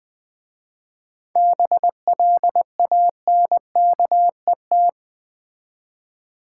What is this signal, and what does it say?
Morse code, heard as a keyed tone, about 21 wpm, 700 Hz. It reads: BLANKET
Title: Easy Words with Hints - 20wpm